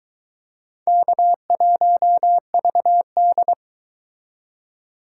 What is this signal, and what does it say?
Morse code, heard as a keyed tone, about 23 wpm, 700 Hz. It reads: K1VD